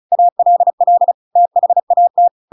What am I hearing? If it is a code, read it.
Morse code, heard as a keyed tone, about 34 wpm, 700 Hz. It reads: ALL THAT
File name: Pinocchio / Pinocchio ch7 35 35 ik1hge